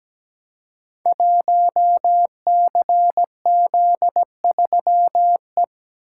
Morse code, heard as a keyed tone, about 17 wpm, 700 Hz. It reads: 1CZ3E
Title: Code Groups - 5 Letter:Numbers - 17wpm